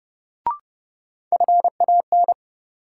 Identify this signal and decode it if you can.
Morse code, heard as a keyed tone, about 30 wpm, 700 Hz. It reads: FAN
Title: Sentences from Top 1000 Words - Review Part 4 - New Word Drill - 30wpm